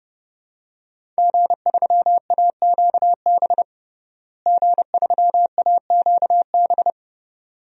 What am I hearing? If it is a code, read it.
Morse code, heard as a keyed tone, about 30 wpm, 700 Hz. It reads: G3AQB G3AQB
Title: DXCC-100 Call Signs - 5 Characters - Repeated 2x - 30wpm